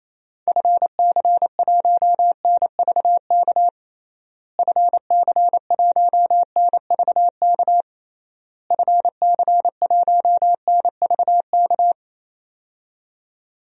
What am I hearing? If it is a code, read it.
Morse code, heard as a keyed tone, about 28 wpm, 700 Hz. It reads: FC1NVK FC1NVK FC1NVK
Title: DXCC-100 Call Signs - 6 Characters - Repeated 3x - 28wpm